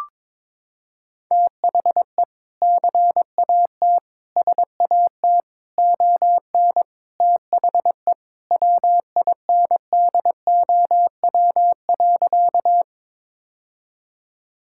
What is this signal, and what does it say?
Morse code, heard as a keyed tone, about 22 wpm, 700 Hz. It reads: THE CAT SAT ON THE WINDOW.
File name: practice / 22wpm / Sentences with 2-Word Phrases - 22wpm